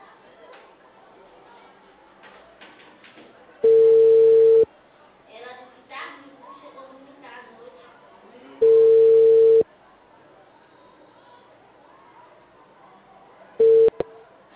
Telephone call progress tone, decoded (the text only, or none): ringback tone